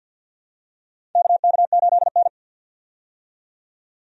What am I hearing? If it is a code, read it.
Morse code, heard as a keyed tone, about 53 wpm, 700 Hz. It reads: KK8N
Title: USA Call Signs - 4 Characters - 50wpm